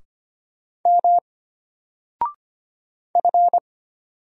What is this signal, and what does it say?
Morse code, heard as a keyed tone, about 25 wpm, 700 Hz. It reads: M F
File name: Single-Letter-Number-25wpm